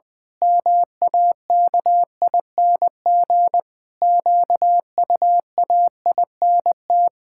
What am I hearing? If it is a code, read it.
Morse code, heard as a keyed tone, about 20 wpm, 700 Hz. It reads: MAKING QUAINT
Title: Pangram Challenge v2 - 20wpm